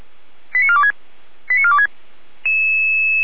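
Five-tone selective call (ZVEI), heard as quick runs of five tones, tones of about 70 ms. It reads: 89426 89426
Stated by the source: ZVEI five-tone selective call received on an SDR, archived by Feuerwehr Orlamünde.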